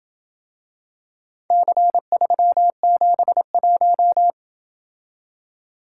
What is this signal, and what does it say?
Morse code, heard as a keyed tone, about 27 wpm, 700 Hz. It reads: C371